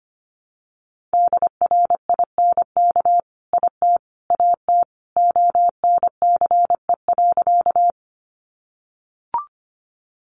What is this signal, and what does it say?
Morse code, heard as a keyed tone, about 25 wpm, 700 Hz. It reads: DRINK IT AT ONCE.
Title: Sentences from Top 1500 Words 25wpm